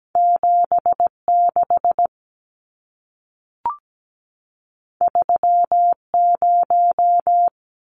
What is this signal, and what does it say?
Morse code, heard as a keyed tone, about 17 wpm, 700 Hz. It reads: 76 30
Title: Numbers - 2-Digit 17wpm